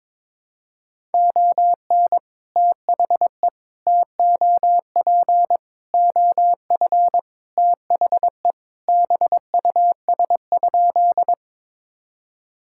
Morse code, heard as a keyed tone, about 22 wpm, 700 Hz. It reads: ON THE TOP OF THE BUS?
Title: Sentences from Top 1000 Words - Review Part 6 - 22wpm